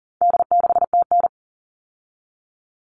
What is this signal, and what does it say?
Morse code, heard as a keyed tone, about 40 wpm, 700 Hz. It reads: D6TN
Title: All Cut Numbers - 4-Digit 40wpm